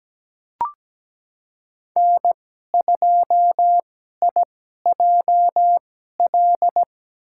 Morse code, heard as a keyed tone, about 17 wpm, spaced as for 12 wpm, 700 Hz. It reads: N2IJL
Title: USA Call Signs - 5 Characters - 2x Character Spacing - 17wpm